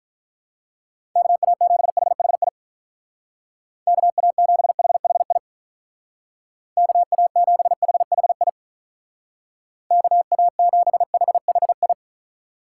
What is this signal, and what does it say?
Morse code, heard as a keyed tone, about 43 wpm, 700 Hz. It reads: KA7HHI KA7HHI KA7HHI KA7HHI
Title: Speed Racing - USA Call Signs 35wpm